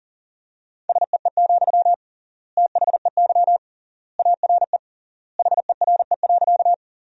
Morse code, heard as a keyed tone, about 40 wpm, 700 Hz. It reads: SEE, THEY ARE HERE.
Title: 3x Word-Spacing - Sentences from Top 200 Words - 40wpm